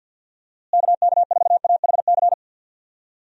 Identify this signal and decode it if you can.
Morse code, heard as a keyed tone, about 52 wpm, 700 Hz. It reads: KK4AHC
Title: USA Call Signs - 6 Characters - Repeated 3x - 50wpm